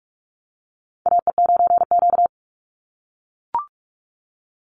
Morse code, heard as a keyed tone, about 44 wpm, 700 Hz. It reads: AE9Q
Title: NA SOTA Chaser Callsigns 45wpm